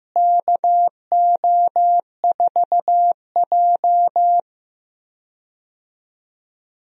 Morse code, heard as a keyed tone, about 15 wpm, 700 Hz. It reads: KO4J
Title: USA Call Signs - 4 Characters - Repeated 2x - 15wpm